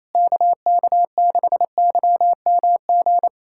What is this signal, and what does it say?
Morse code, heard as a keyed tone, about 28 wpm, 700 Hz. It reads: KK6YMG
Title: USA Callsign DE Callsign 28wpm